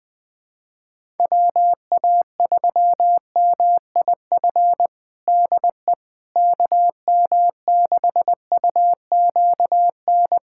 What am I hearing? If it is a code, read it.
Morse code, heard as a keyed tone, about 21 wpm, 700 Hz. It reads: WA3MIF DE KM6UQN
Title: USA Callsign DE Callsign - 6 Characters - 20wpm